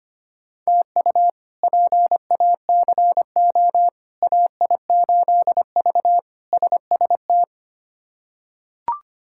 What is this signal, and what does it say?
Morse code, heard as a keyed tone, about 25 wpm, 700 Hz. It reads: TU PACO AI8V SST
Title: K1USN SST QSO Practice - 25wpm